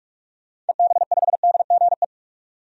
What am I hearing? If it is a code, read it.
Morse code, heard as a keyed tone, about 46 wpm, 700 Hz. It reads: EB5DGE
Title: DXCC-100 Call Signs - 6 Characters - 45wpm